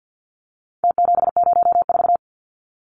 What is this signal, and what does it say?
Morse code, heard as a keyed tone, about 49 wpm, 700 Hz. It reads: T704
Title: Numbers - Contest Style - 4-Digit 50wpm